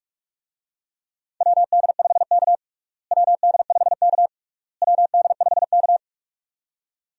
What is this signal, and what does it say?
Morse code, heard as a keyed tone, about 46 wpm, 700 Hz. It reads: WD5K WD5K WD5K
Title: USA Call Signs - 4 Characters - Repeated 3x - 45wpm